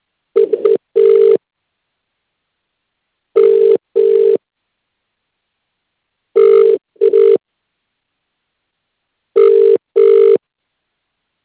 Ringback tone (UK), 400 + 450 Hz.